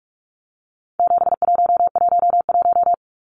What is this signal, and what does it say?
Morse code, heard as a keyed tone, about 44 wpm, 700 Hz. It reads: 7111
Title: Numbers - 4-Digit 45wpm